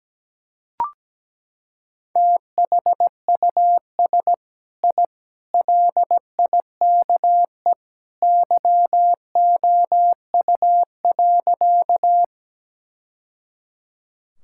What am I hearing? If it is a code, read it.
Morse code, heard as a keyed tone, about 17 wpm, 700 Hz. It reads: THUS I LIKE YOU.
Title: Sentences from Top 1000 Words - Review Part 5 - 17wpm